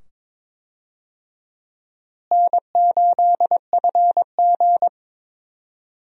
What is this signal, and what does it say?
Morse code, heard as a keyed tone, about 22 wpm, 700 Hz. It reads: N8FG
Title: USA Call Signs - 4 Characters - Repeated 2x - 22wpm